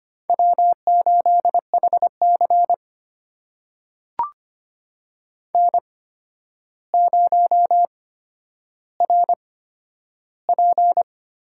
Morse code, heard as a keyed tone, about 25 wpm, spaced as for 11 wpm, 700 Hz. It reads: W8HC N0RP
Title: USA Call Signs - 4 Characters - 8x Character Spacing - 25wpm